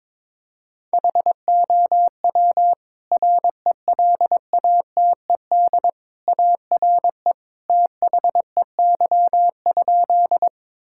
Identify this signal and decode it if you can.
Morse code, heard as a keyed tone, about 22 wpm, 700 Hz. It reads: HOW RELATED ARE THEY?
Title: Sentences from Top 1000 Words - Review Part 5 - 22wpm